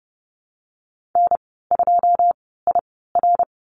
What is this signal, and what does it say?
Morse code, heard as a keyed tone, about 30 wpm, spaced as for 17 wpm, 700 Hz. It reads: N2IR
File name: Lesson 15g - Number 5 - Call Signs 30wpm